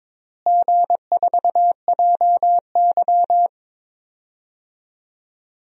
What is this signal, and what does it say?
Morse code, heard as a keyed tone, about 22 wpm, 700 Hz. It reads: G4JY
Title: DXCC-100 Call Signs - 4 Characters - Repeated 3x - 22wpm